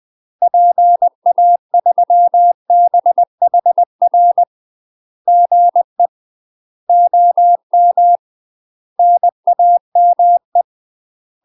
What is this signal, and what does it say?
Morse code, heard as a keyed tone, about 20 wpm, 700 Hz. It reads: PA3BHR GE OM NAME
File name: UK MASTER QSOs TRAINING / 20WPM / UK MASTER QSOs 0026